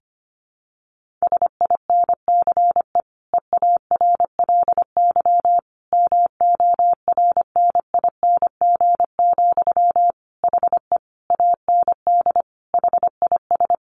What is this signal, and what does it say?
Morse code, heard as a keyed tone, about 25 wpm, 700 Hz. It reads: SINCE EARLY MORNING, HE AND HIS
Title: Sentences from Top 1400 Words 25wpm